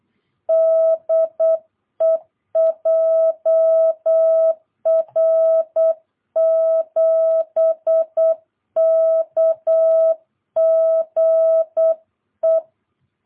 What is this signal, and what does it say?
Morse code, heard as a keyed tone, about 8 wpm, 650 Hz. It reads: DE JR7KGE
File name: jh8cbh / 2022cw